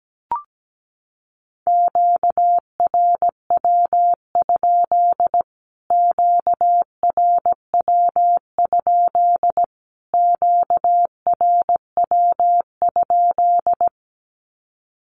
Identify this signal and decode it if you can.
Morse code, heard as a keyed tone, about 17 wpm, 700 Hz. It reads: QRW? QRW? QRW?
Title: Q-Codes Repeated 3x 17wpm